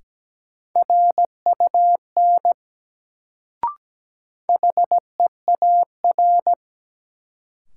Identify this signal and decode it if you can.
Morse code, heard as a keyed tone, about 17 wpm, 700 Hz. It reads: RUN HEAR
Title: Sentences from Top 300 Words - New Word Drill - 17wpm